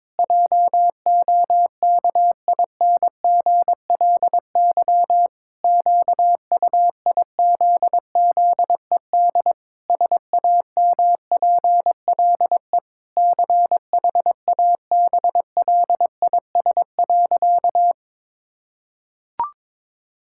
Morse code, heard as a keyed tone, about 22 wpm, 700 Hz. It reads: JOKINGLY QUIZZED SAMPLE CHABLIS.
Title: Pangram Challenge v2 - 22wpm